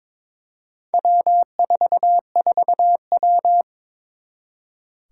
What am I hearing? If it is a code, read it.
Morse code, heard as a keyed tone, about 22 wpm, 700 Hz. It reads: W44W
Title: Four Challenging Letter-Number Combinations 22wpm